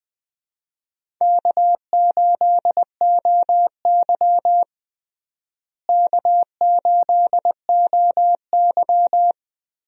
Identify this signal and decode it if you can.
Morse code, heard as a keyed tone, about 20 wpm, 700 Hz. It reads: K8OY K8OY
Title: USA Call Signs - 4 Characters - Repeated 3x - 20wpm